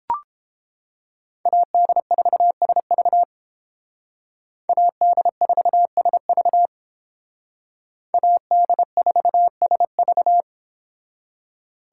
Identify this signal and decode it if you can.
Morse code, heard as a keyed tone, about 29 wpm, 700 Hz. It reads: AD4SV AD4SV AD4SV